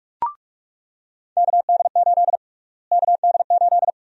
Morse code, heard as a keyed tone, about 43 wpm, 700 Hz. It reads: KD8 KD8
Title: USA Call Sign Prefixes - Repeated 2x - 45wpm